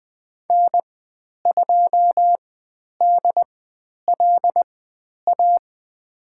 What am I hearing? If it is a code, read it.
Morse code, heard as a keyed tone, about 20 wpm, spaced as for 10 wpm, 700 Hz. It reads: N2DLA
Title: 102 Lesson 17g - Letter u - Call Signs 20wpm